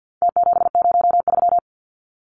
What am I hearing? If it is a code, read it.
Morse code, heard as a keyed tone, about 48 wpm, 700 Hz. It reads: T703
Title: Numbers - Contest Style - 4-Digit 50wpm